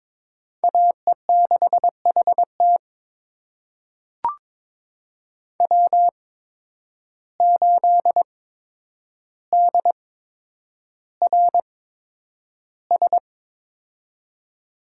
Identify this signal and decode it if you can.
Morse code, heard as a keyed tone, about 21 wpm, spaced as for 9 wpm, 700 Hz. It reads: AE6HT W8DRS
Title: USA Call Signs - 5 Characters - 8x Character Spacing - 22wpm